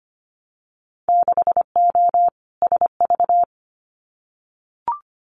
Morse code, heard as a keyed tone, about 25 wpm, 700 Hz. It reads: 6O SV
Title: Winter Field Day Exchanges Encore 25wpm